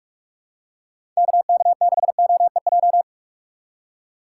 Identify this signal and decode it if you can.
Morse code, heard as a keyed tone, about 47 wpm, 700 Hz. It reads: KK6OEJ
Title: USA Call Signs - 3x Character Spacing - 45wpm